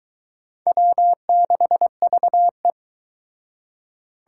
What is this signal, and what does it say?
Morse code, heard as a keyed tone, about 22 wpm, 700 Hz. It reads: W6VE